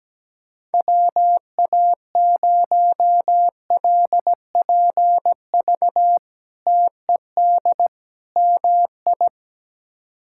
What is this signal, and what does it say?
Morse code, heard as a keyed tone, about 17 wpm, 700 Hz. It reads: WA0LPV TED MI
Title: K1USN SST QSO Practice - 17wpm